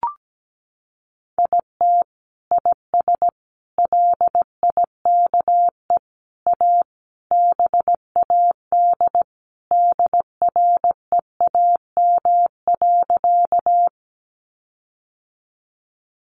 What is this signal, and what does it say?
Morse code, heard as a keyed tone, about 17 wpm, 700 Hz. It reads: IT IS LIKE A BAD DREAM.
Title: Sentences from Top 1500 Words 17wpm